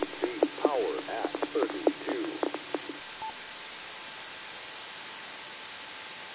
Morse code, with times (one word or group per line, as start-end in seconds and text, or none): none